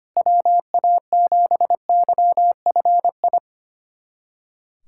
Morse code, heard as a keyed tone, about 25 wpm, 700 Hz. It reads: WA7YFI
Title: USA Callsign DE Callsign - 6 Characters - 25wpm